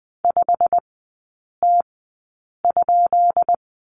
Morse code, heard as a keyed tone, about 20 wpm, spaced as for 8 wpm, 700 Hz. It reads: 5T?
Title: 212 Lesson 34e - Number 8 - Three Characters 20wpm